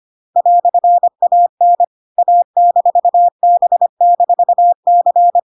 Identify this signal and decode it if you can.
Morse code, heard as a keyed tone, about 25 wpm, 700 Hz. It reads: "AN A-B-C